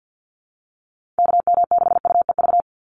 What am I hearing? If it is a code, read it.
Morse code, heard as a keyed tone, about 48 wpm, 700 Hz. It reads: KM6UEV
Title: USA Call Signs 50wpm